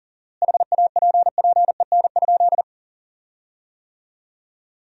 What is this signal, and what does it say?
Morse code, heard as a keyed tone, about 38 wpm, 700 Hz. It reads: HAPPEN?
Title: 4x Word-Spacing - Sentences from Top 400 Words - 40wpm